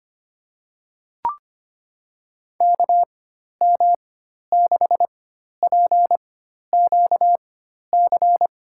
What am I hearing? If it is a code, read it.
Morse code, heard as a keyed tone, about 25 wpm, spaced as for 12 wpm, 700 Hz. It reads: KM6PQC
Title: USA Call Signs - 6 Characters - 4x Character Spacing - 25wpm